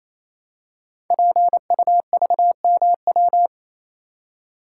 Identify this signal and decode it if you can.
Morse code, heard as a keyed tone, about 28 wpm, 700 Hz. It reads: PUVMW
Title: Code Groups - 5 Letters - 28wpm